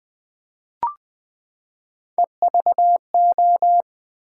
Morse code, heard as a keyed tone, about 20 wpm, 700 Hz. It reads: EVO
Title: Build A Word - 3 Letters At A Time - 6-Letter Words - 20wpm